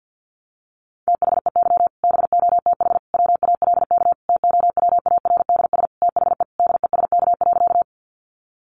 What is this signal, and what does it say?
Morse code, heard as a keyed tone, about 50 wpm, 700 Hz. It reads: THEY BOTH WALK TOWARDS THE DESK.
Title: Additional Easy Phrases 50wpm